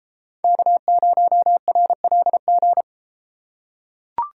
Morse code, heard as a keyed tone, about 32 wpm, 700 Hz. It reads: K0RLG